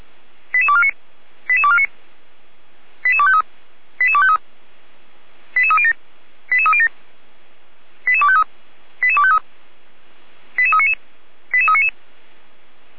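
Five-tone selective call (ZVEI), heard as quick runs of five tones, tones of about 70 ms. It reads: 88249 88249 88253 88253 88307 88307 88253 88253 88399 88399
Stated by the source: ZVEI five-tone selective call received on an SDR.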